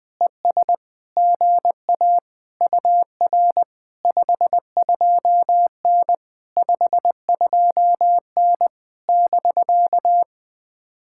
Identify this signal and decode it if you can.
Morse code, heard as a keyed tone, about 20 wpm, 700 Hz. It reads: ES GA UR 52N 52N <BK>
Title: POTA QSO Practice - 20wpm